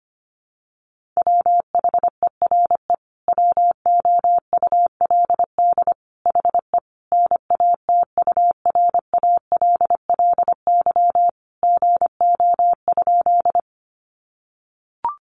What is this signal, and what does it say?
Morse code, heard as a keyed tone, about 25 wpm, 700 Hz. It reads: WHERE WOULD HE NATURALLY GO?
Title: Sentences from Top 1300 Words 25wpm